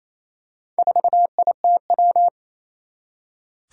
Morse code, heard as a keyed tone, about 27 wpm, 700 Hz. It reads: 4ITW
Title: Four Challenging Letter-Number Combinations 28wpm